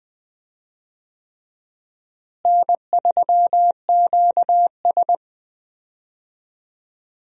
Morse code, heard as a keyed tone, about 20 wpm, 700 Hz. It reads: N3QS
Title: USA Call Signs - 4 Characters - Repeated 3x - 20wpm